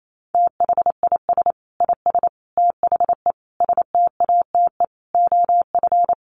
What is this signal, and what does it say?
Morse code, heard as a keyed tone, about 28 wpm, 700 Hz. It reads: THIS IS THE STATE OF
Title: Sentences from Top 1400 Words 28wpm